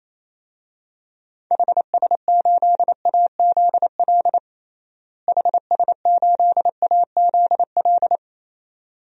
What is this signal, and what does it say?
Morse code, heard as a keyed tone, about 28 wpm, 700 Hz. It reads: HS8AZL HS8AZL
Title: DXCC-100 Call Signs - 6 Characters - Repeated 2x - 28wpm